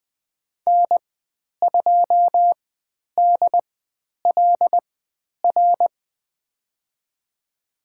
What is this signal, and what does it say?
Morse code, heard as a keyed tone, about 20 wpm, spaced as for 10 wpm, 700 Hz. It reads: N2DLR
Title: 102 Lesson 17g - Letter u - Call Signs 20wpm